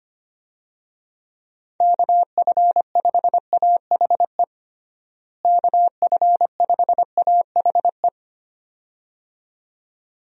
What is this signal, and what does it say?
Morse code, heard as a keyed tone, about 25 wpm, 700 Hz. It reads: KF5AHE KF5AHE
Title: USA Call Signs - 6 Characters - Repeated 2x - 25wpm